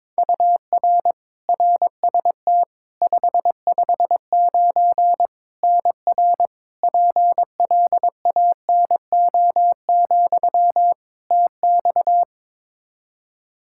Morse code, heard as a keyed tone, about 22 wpm, 700 Hz. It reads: UR RST 559 NR PLANO, TX
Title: Practice QSO - Exchange 1 - Beginning - 22wpm